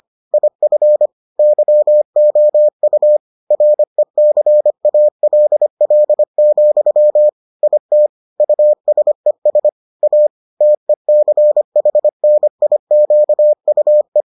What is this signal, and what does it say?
Morse code, heard as a keyed tone, about 25 wpm, 600 Hz. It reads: IF YOU RECALL, IT USES A TECHNIQUE